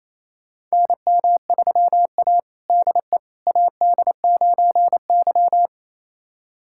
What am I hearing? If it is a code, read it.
Morse code, heard as a keyed tone, about 28 wpm, 700 Hz. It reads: NM3A DE AD9Y
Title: USA Callsign DE Callsign - 4 Characters - 28wpm